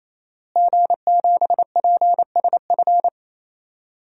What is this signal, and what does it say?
Morse code, heard as a keyed tone, about 28 wpm, 700 Hz. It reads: G7PSF
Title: DXCC-100 Call Signs - 5 Characters - Repeated 3x - 28wpm